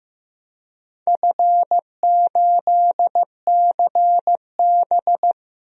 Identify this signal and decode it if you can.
Morse code, heard as a keyed tone, about 15 wpm, 700 Hz. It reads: F8CB
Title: DXCC-100 Call Signs - 4 Characters - 15wpm